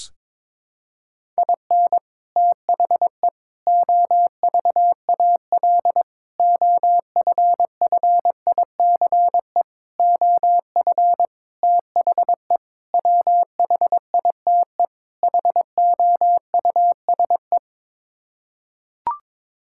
Morse code, heard as a keyed tone, about 22 wpm, 700 Hz. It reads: IN THE OVAL OFFICE OF THE WHITE HOUSE
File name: practice / 22wpm / Sets of 8 Words - 22wpm